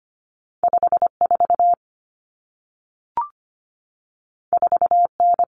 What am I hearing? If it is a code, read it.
Morse code, heard as a keyed tone, about 25 wpm, 700 Hz. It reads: 54 4N
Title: Common Cut Numbers - 2-Digit 25wpm